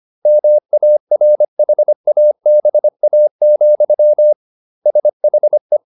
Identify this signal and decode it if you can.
Morse code, heard as a keyed tone, about 25 wpm, 600 Hz. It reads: MARHABA, SHE